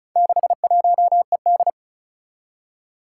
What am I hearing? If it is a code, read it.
Morse code, heard as a keyed tone, about 36 wpm, 700 Hz. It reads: 61ED
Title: Four Challenging Letter-Number Combinations 35wpm